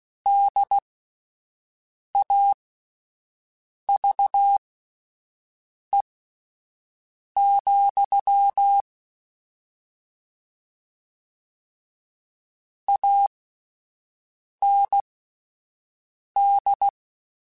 Morse code, heard as a keyed tone, about 16 wpm, spaced as for 5 wpm, 800 Hz. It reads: DAVE, AND
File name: (spaced) / QSOs Spaced-mp3s / WA7QOX 2 @10 sp